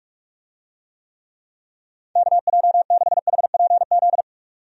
Morse code, heard as a keyed tone, about 42 wpm, 700 Hz. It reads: KJ6HPZ
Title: USA Call Signs - 6 Characters - 45wpm